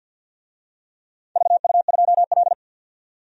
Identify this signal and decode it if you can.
Morse code, heard as a keyed tone, about 51 wpm, 700 Hz. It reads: VU2L
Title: DXCC-100 Call Signs - 4 Characters - 50wpm